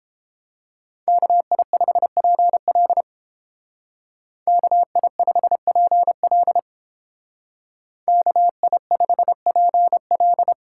Morse code, heard as a keyed tone, about 31 wpm, 700 Hz. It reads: KI5PL KI5PL KI5PL